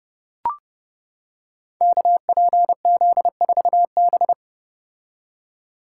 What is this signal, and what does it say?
Morse code, heard as a keyed tone, about 30 wpm, 700 Hz. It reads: KPZ4B
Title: Code Groups - 5 Letter:Numbers - 30wpm